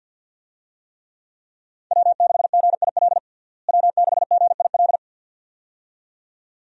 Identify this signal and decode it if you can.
Morse code, heard as a keyed tone, about 47 wpm, 700 Hz. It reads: W6GIL W6GIL
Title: USA Call Signs - 5 Characters - Repeated 2x - 50wpm